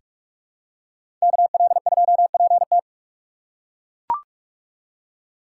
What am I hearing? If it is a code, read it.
Morse code, heard as a keyed tone, about 42 wpm, 700 Hz. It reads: KL2PT
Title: USA Call Signs - 5 Characters - 4x Character Spacing - 45wpm